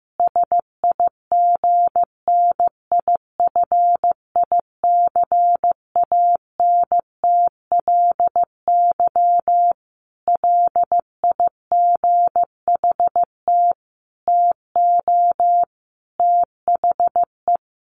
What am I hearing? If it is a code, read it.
Morse code, heard as a keyed tone, about 15 wpm, 700 Hz. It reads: SIGNIFICANTLY LIGHT TO THE
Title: Sentences from Top 1500 Words 15wpm